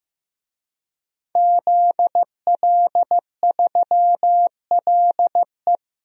Morse code, heard as a keyed tone, about 15 wpm, 700 Hz. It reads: ZL3LE
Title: DXCC-100 Call Signs - 5 Characters - Repeated 2x - 15wpm